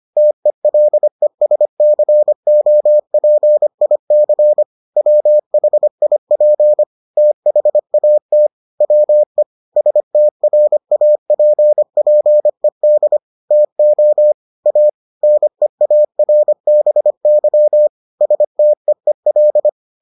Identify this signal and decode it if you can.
Morse code, heard as a keyed tone, about 25 wpm, 600 Hz. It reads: TELESCOPIC WHIP THAT WE STRAPPED TO A NEARBY STEEL